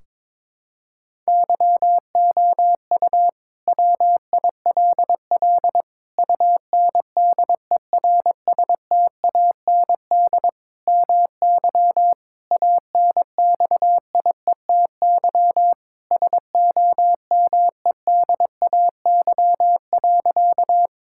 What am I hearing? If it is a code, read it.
Morse code, heard as a keyed tone, about 22 wpm, 700 Hz. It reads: YOU WILL UNDERSTAND MY ANXIETY SOMEDAY.
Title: Sentences from Top 1600 Words 22wpm